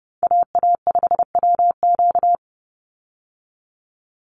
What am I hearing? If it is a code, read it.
Morse code, heard as a keyed tone, about 30 wpm, 700 Hz. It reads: AA5WQ
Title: USA Call Signs 30wpm